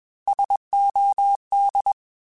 Morse code, heard as a keyed tone, about 21 wpm, 800 Hz. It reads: SOD
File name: Alphabetical / Alphabetical mp3s / Ss @20